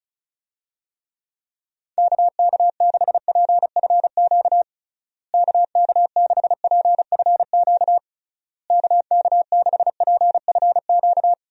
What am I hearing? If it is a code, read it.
Morse code, heard as a keyed tone, about 36 wpm, 700 Hz. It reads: KK6PFQ KK6PFQ KK6PFQ
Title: USA Call Signs - 6 Characters - Repeated 3x - 35wpm